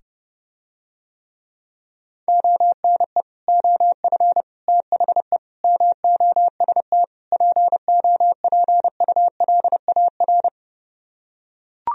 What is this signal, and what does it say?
Morse code, heard as a keyed tone, about 30 wpm, 700 Hz. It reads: ONE OF THE MOST POPULAR